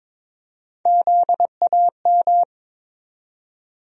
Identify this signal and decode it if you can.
Morse code, heard as a keyed tone, about 22 wpm, 700 Hz. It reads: ZAM
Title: USA Call Sign Postfixes - 3 Letters 22wpm